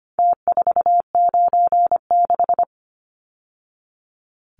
Morse code, heard as a keyed tone, about 25 wpm, 700 Hz. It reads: T496
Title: Numbers - Contest Style - 4-Digit 25wpm